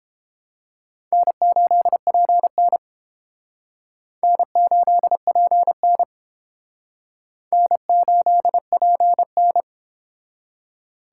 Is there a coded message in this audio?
Yes: Morse code from 1.1 to 9.6 s, heard as a keyed tone, about 29 wpm, 700 Hz. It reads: N8PN N8PN N8PN